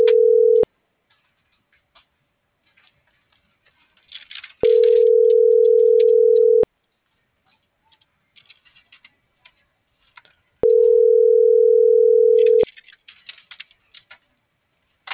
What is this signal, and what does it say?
Ringback tone (North America), 440 + 480 Hz.